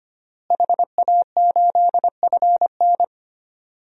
Morse code, heard as a keyed tone, about 25 wpm, 700 Hz. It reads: HA8FN
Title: DXCC-100 Call Signs - 5 Characters - Repeated 3x - 25wpm